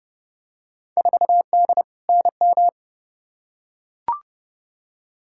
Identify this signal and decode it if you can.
Morse code, heard as a keyed tone, about 30 wpm, 700 Hz. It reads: 4D NM
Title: Field Day Exchanges Encore - 30wpm